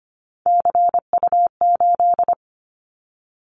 Morse code, heard as a keyed tone, about 25 wpm, 700 Hz. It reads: CU8
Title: Callsign Prefixes for Top 50 DXCC - 25wpm